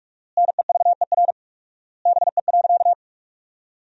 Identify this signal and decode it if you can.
Morse code, heard as a keyed tone, about 46 wpm, 700 Hz. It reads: NEVER BE.
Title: 4x Word-Spacing - Sentences from Top 300 Words - 45wpm